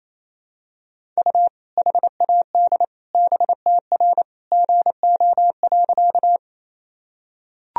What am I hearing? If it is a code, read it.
Morse code, heard as a keyed tone, about 27 wpm, 700 Hz. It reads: U HAD BTR GO.
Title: Sentences from Top 400 Words and Abbreviations 28wpm